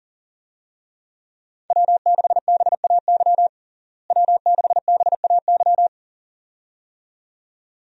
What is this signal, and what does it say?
Morse code, heard as a keyed tone, about 40 wpm, 700 Hz. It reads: W6BAY W6BAY
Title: USA Call Signs - 5 Characters - Repeated 2x - 40wpm